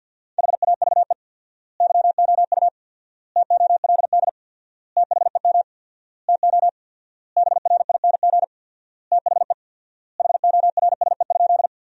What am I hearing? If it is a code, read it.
Morse code, heard as a keyed tone, about 53 wpm, 700 Hz. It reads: HAVE YOU TOLD THEM TO BRING THE HORSE?
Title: 4x Word-Spacing - Sentences from Top 500 Words - 50wpm